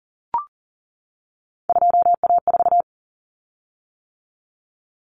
Morse code, heard as a keyed tone, about 40 wpm, 700 Hz. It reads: 2A4
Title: Common Cut Numbers - 3-Digit 40wpm